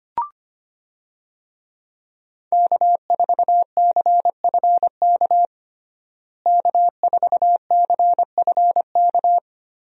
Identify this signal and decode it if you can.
Morse code, heard as a keyed tone, about 25 wpm, 700 Hz. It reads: K4CFK K4CFK
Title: USA Call Signs - 5 Characters - Repeated 3x - 25wpm